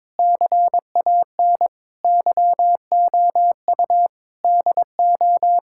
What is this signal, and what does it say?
Morse code, heard as a keyed tone, about 22 wpm, 700 Hz. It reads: CAN YOU DO
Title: Sentences from Top 1600 Words 22wpm